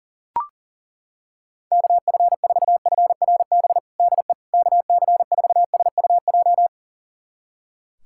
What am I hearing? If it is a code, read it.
Morse code, heard as a keyed tone, about 37 wpm, 700 Hz. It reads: KF4FRB DE KC4SUJ
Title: USA Callsign DE Callsign - 6 Characters - 40wpm